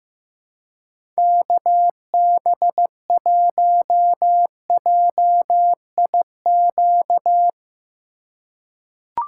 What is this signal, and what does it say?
Morse code, heard as a keyed tone, about 15 wpm, 700 Hz. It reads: KB1JIQ
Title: USA Call Signs - 6 Characters - 4x Character Spacing - 15wpm